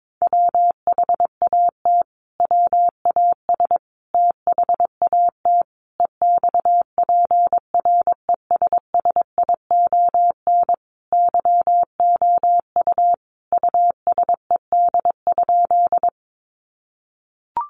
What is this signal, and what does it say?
Morse code, heard as a keyed tone, about 22 wpm, 700 Hz. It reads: WHAT WAS THAT EXPRESSION YOU USED?
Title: Sentences from Top 1200 Words 22wpm